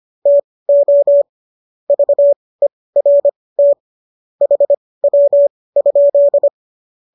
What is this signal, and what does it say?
Morse code, heard as a keyed tone, about 25 wpm, spaced as for 18 wpm, 570 Hz. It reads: TO VERT HW?